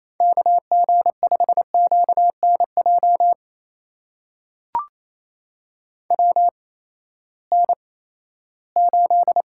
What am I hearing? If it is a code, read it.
Morse code, heard as a keyed tone, about 28 wpm, spaced as for 16 wpm, 700 Hz. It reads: KG5QNJ WN8
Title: USA Call Signs - 8x Character Spacing - 28wpm